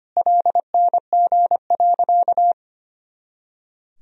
Morse code, heard as a keyed tone, about 25 wpm, 700 Hz. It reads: LNG.
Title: Sentences from Top 500 Words and Abbreviations 25wpm